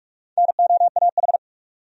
Morse code, heard as a keyed tone, about 46 wpm, 700 Hz. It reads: NOAH
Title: USA Common Names - Max 5 Letters - 45wpm